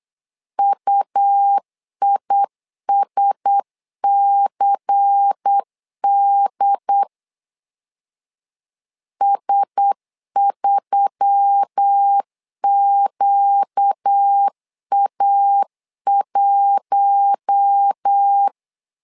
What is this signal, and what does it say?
Morse code, heard as a keyed tone, about 8 wpm, 800 Hz. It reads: UISCD S3QA1
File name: LEZIONI / 5 LBVC / 5 LBVC 8wpm 2935